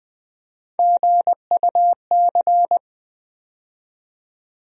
Morse code, heard as a keyed tone, about 20 wpm, 700 Hz. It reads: GUC
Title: USA Call Sign Postfixes - 3 Letters 20wpm